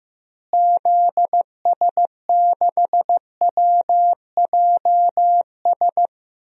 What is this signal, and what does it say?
Morse code, heard as a keyed tone, about 15 wpm, 700 Hz. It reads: ZS6WJS
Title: DXCC-100 Call Signs - 6 Characters - Repeated 3x - 15wpm